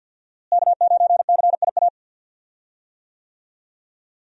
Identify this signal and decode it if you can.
Morse code, heard as a keyed tone, about 46 wpm, 690 Hz. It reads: K9CIA
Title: USA Call Signs - 5 Characters - Repeated 3x - 50wpm